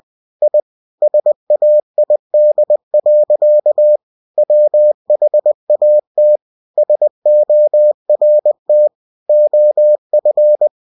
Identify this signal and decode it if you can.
Morse code, heard as a keyed tone, about 20 wpm, 600 Hz. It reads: I SAID. WHAT SORT OF
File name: cw-books / winnie-the-pooh / winnie-20wpm-page-0001